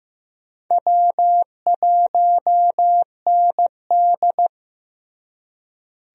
Morse code, heard as a keyed tone, about 15 wpm, 700 Hz. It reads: W1ND
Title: USA Call Signs - 4 Characters - Repeated 2x - 15wpm